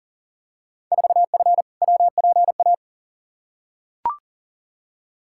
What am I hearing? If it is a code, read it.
Morse code, heard as a keyed tone, about 38 wpm, 700 Hz. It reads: 4F WPA
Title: Field Day Exchanges - 40wpm